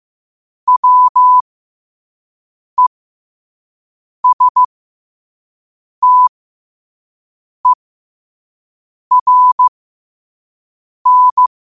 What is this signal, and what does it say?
Morse code, heard as a keyed tone, about 15 wpm, spaced as for 5 wpm, 1000 Hz. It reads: WESTERN
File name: qotdfiles / 5wpm